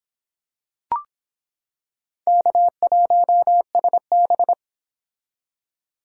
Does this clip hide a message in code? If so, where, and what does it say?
Yes: Morse code from 2.3 to 4.5 s, heard as a keyed tone, about 26 wpm, 700 Hz. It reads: K1SB